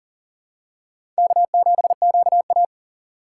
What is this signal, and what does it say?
Morse code, heard as a keyed tone, about 40 wpm, 700 Hz. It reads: K7QA
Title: USA Call Signs - 4 Characters - 4x Character Spacing - 40wpm